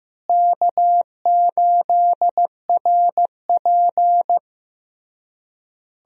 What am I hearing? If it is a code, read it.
Morse code, heard as a keyed tone, about 15 wpm, 700 Hz. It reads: K8RP
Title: USA Call Signs - 4 Characters - Repeated 3x - 15wpm